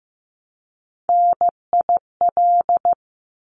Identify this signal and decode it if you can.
Morse code, heard as a keyed tone, about 15 wpm, 700 Hz. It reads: NIL
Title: Abbreviations Repeated 3x - 15wpm